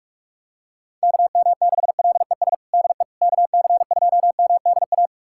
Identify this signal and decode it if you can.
Morse code, heard as a keyed tone, about 46 wpm, 700 Hz. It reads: KM6LES DE KC2MDA
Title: USA Callsign DE Callsign - 6 Characters - 45wpm